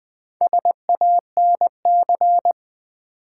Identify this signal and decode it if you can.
Morse code, heard as a keyed tone, about 20 wpm, 700 Hz. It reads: SANC
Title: Build A Word - 2 Letters At A Time - 8-Letter Words - 20wpm